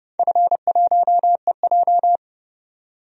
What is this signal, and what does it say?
Morse code, heard as a keyed tone, about 30 wpm, 700 Hz. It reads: F1EJ
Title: DXCC-100 Call Signs - 4 Characters - 30wpm